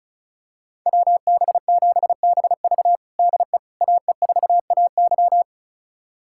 Morse code, heard as a keyed tone, about 36 wpm, 700 Hz. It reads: WB7BV DE AE4AY
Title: USA Callsign DE Callsign - 5 Characters - 35wpm